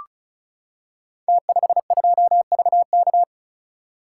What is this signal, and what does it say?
Morse code, heard as a keyed tone, about 36 wpm, 700 Hz. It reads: T52VK
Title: Code Groups - 5 Letter:Numbers - 35wpm